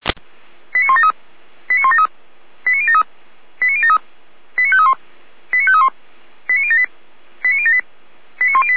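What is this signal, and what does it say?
Five-tone selective call (ZVEI), heard as quick runs of five tones, tones of about 70 ms. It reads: 89173 89173 89073 89073 89531 89531 89078 89078 89108
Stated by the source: ZVEI five-tone selective call received with an SDR, archived by Feuerwehr Orlamünde.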